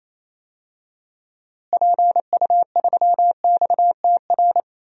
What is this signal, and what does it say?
Morse code, heard as a keyed tone, about 28 wpm, 700 Hz. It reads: PU3XTR
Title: DXCC-100 Call Signs - 6 Characters - Repeated 2x - 28wpm